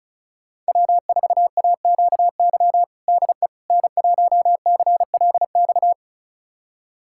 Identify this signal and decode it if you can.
Morse code, heard as a keyed tone, about 34 wpm, 700 Hz. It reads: W4AQY DE N1CLX
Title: USA Callsign DE Callsign - 5 Characters - 35wpm